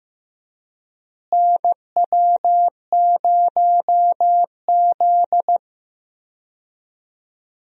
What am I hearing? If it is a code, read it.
Morse code, heard as a keyed tone, about 15 wpm, 700 Hz. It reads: NW0Z
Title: USA Call Signs - 4 Characters - Repeated 3x - 15wpm